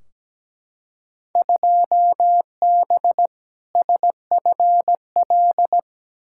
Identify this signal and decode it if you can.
Morse code, heard as a keyed tone, about 17 wpm, 700 Hz. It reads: 2B SFL
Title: Field Day Exchanges - 17wpm